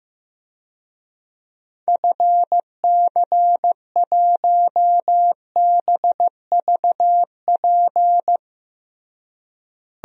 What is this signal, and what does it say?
Morse code, heard as a keyed tone, about 15 wpm, 700 Hz. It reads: FC1BVP
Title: DXCC-100 Call Signs - 6 Characters - Repeated 3x - 15wpm